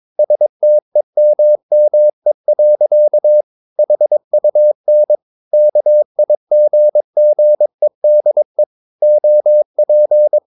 Morse code, heard as a keyed tone, about 22 wpm, 600 Hz. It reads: STEMME. HUN KIGGEDE OP
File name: Ferie QRS 22wpm